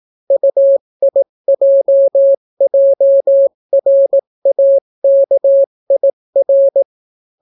Morse code, heard as a keyed tone, about 18 wpm, 550 Hz. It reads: UIJJRAKIR